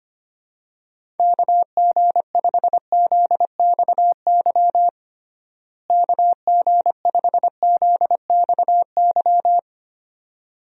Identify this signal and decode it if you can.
Morse code, heard as a keyed tone, about 25 wpm, 700 Hz. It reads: KG5ZXY KG5ZXY
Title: USA Call Signs - 6 Characters - Repeated 2x - 25wpm